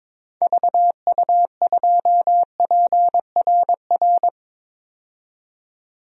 Morse code, heard as a keyed tone, about 22 wpm, 700 Hz. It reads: VU2PRR
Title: DXCC-100 Call Signs - 6 Characters - 22wpm